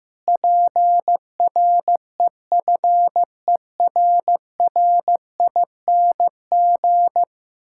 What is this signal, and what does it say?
Morse code, heard as a keyed tone, about 15 wpm, 700 Hz. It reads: PREFERRING
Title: Pangram Challenge v2 - 15wpm